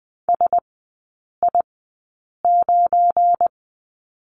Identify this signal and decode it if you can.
Morse code, heard as a keyed tone, about 20 wpm, spaced as for 8 wpm, 700 Hz. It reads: SI9
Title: 184 Lesson 30e - Letter b - Three Characters 20wpm